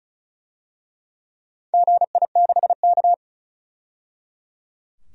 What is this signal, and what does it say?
Morse code, heard as a keyed tone, about 36 wpm, 690 Hz. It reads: GI6K
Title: DXCC-100 Call Signs - 4 Characters - 35wpm